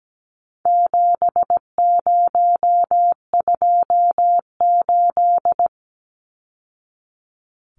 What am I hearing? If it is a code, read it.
Morse code, heard as a keyed tone, about 17 wpm, 700 Hz. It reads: 7028
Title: Numbers - 4-Digit 17wpm